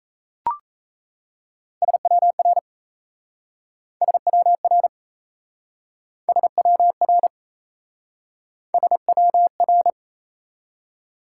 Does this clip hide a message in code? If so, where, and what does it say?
Yes: Morse code from 1.8 to 9.9 s, heard as a keyed tone, about 38 wpm, 700 Hz. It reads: SWR SWR SWR SWR